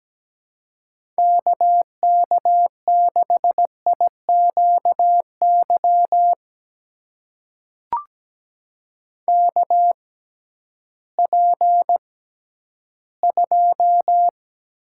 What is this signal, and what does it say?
Morse code, heard as a keyed tone, about 17 wpm, spaced as for 11 wpm, 700 Hz. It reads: KK6IQY KP2
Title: USA Call Signs - 6x Character Spacing - 17wpm